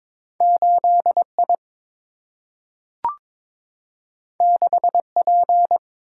Morse code, heard as a keyed tone, about 22 wpm, 700 Hz. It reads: 8 I 6 P
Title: Single-Letter Single-Number 22wpm